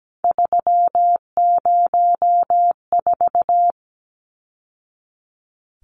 Morse code, heard as a keyed tone, about 17 wpm, 700 Hz. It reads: 304
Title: Numbers - 3-Digit 17wpm